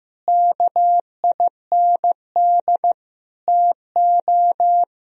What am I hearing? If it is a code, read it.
Morse code, heard as a keyed tone, about 15 wpm, 700 Hz. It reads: KIND TO
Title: Sentences from Top 1000 Words - Review Part 2 - 15wpm